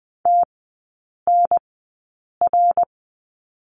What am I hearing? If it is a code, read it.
Morse code, heard as a keyed tone, about 20 wpm, spaced as for 8 wpm, 700 Hz. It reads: TNR
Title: 053 Lesson 10e - Letter r - Three Characters 20wpm